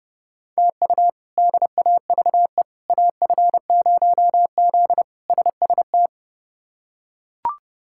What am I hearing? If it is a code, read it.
Morse code, heard as a keyed tone, about 32 wpm, 700 Hz. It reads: TU DAVE AF0Z SST
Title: K1USN SST QSO Practice - 30wpm